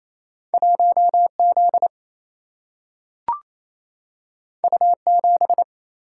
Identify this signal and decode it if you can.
Morse code, heard as a keyed tone, about 28 wpm, 700 Hz. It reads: 1Z U7
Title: Single-Letter Single-Number 28wpm